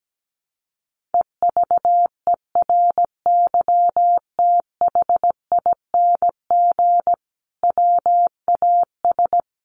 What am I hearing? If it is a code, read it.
Morse code, heard as a keyed tone, about 17 wpm, 700 Hz. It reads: EVERYTHING WAS
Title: Sentences from Top 700 Words 17wpm